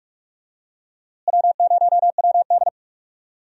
Morse code, heard as a keyed tone, about 44 wpm, 700 Hz. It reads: W0WD
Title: USA Call Signs - 4 Characters - Repeated 2x - 45wpm